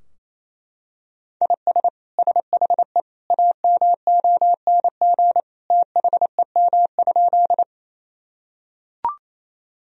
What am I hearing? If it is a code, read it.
Morse code, heard as a keyed tone, about 28 wpm, 700 Hz. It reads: IS SHE AMONG THEM?
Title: Sentences from Top 1000 Words - Review Part 3 - 28wpm